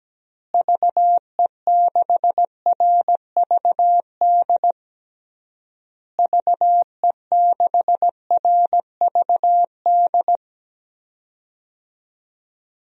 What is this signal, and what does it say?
Morse code, heard as a keyed tone, about 17 wpm, 700 Hz. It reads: VE6RVD VE6RVD
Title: DXCC-100 Call Signs - 6 Characters - Repeated 3x - 17wpm